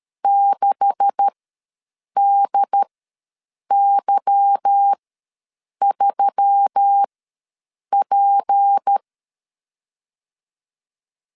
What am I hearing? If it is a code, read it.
Morse code, heard as a keyed tone, about 13 wpm, spaced as for 7 wpm, 800 Hz. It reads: 6DY3P
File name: LEZIONI / 9 286 ws 12wpm 3351